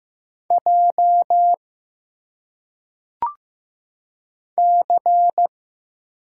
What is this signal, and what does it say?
Morse code, heard as a keyed tone, about 15 wpm, 700 Hz. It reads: J C
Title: Single Letters - Phonetic Pronunciation 15wpm